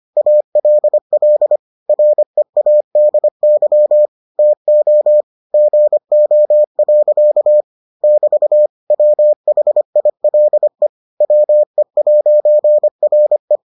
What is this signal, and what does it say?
Morse code, heard as a keyed tone, about 25 wpm, 600 Hz. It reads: ALL READY TO GO. <BT> WHILE WE'RE